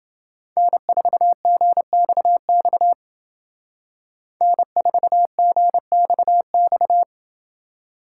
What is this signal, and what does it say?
Morse code, heard as a keyed tone, about 28 wpm, 700 Hz. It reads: N4GXX N4GXX